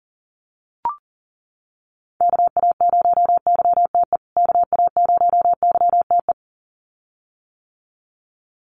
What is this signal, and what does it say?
Morse code, heard as a keyed tone, about 39 wpm, 700 Hz. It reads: KA0YTE KA0YTE
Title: USA Call Signs Repeated 2x 40wpm